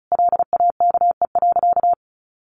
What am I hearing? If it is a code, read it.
Morse code, heard as a keyed tone, about 34 wpm, 700 Hz. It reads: LAKE.
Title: Sentences from Top 1500 Words 35wpm